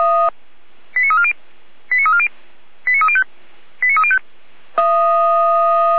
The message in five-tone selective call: 89340 89340 89305 89305

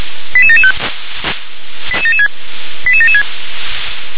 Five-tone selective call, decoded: 88775 88776 88776

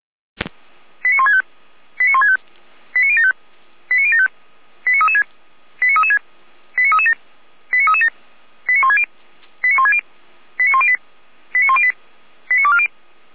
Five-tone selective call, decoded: 89165 89165 89075 89075 89336 89336 89337 89337 89160 89160 89108 89108 89240

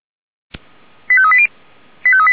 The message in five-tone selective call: 85380

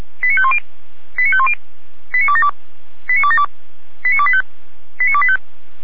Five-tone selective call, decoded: 89510 89510 89272 89272 89285 89285

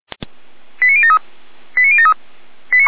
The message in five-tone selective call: 89073 89073

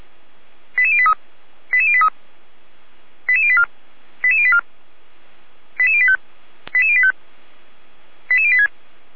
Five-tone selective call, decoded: 88083 88083 88084 88084 88085 88085 88086

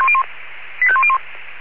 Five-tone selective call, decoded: 85101

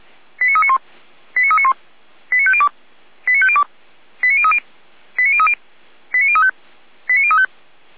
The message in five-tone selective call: 89391 89391 89502 89502 89030 89030 89035 89035

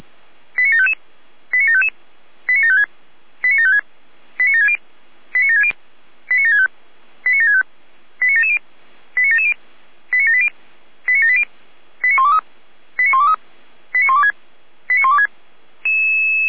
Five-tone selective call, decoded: 89755 89755 89756 89756 89760 89760 89765 89765 89770 89770 89780 89780 89123 89123 89126 89126

